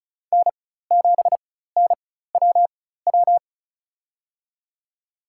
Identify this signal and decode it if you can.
Morse code, heard as a keyed tone, about 36 wpm, spaced as for 16 wpm, 700 Hz. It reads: N7NWW
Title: USA Call Signs - 5 Characters - 4x Character Spacing - 35wpm